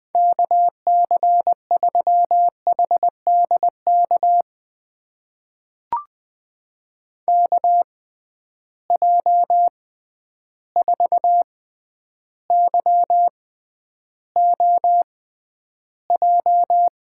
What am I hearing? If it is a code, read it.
Morse code, heard as a keyed tone, about 21 wpm, spaced as for 10 wpm, 700 Hz. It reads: KC3HDK KJ4YOJ
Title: USA Call Signs - 6 Characters - 6x Character Spacing - 20wpm